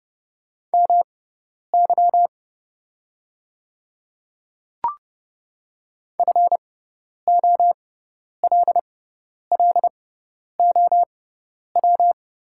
Morse code, heard as a keyed tone, about 30 wpm, spaced as for 10 wpm, 700 Hz. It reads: MY FOLLOW
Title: 6x Character Spacing - Top 200 Words - 30wpm